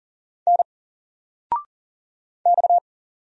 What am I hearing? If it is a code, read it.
Morse code, heard as a keyed tone, about 40 wpm, 700 Hz. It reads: N X
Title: Single Letter-Number Alt 40wpm